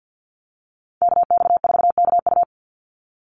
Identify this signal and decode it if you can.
Morse code, heard as a keyed tone, about 50 wpm, 700 Hz. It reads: KX4KU
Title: Call Signs - 50wpm